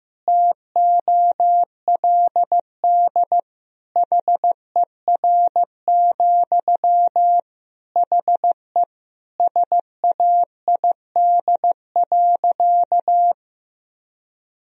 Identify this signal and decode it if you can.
Morse code, heard as a keyed tone, about 15 wpm, 700 Hz. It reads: TOLD HER, HE SAID.
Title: Sentences from Top 1000 Words - Review Part 4 - 15wpm